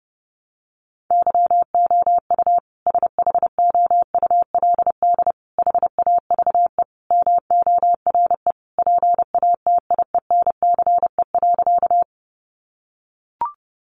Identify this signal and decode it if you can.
Morse code, heard as a keyed tone, about 30 wpm, 700 Hz. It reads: YOU SHOULD HAVE MORE PATIENCE.
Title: Sentences from Top 1300 Words 30wpm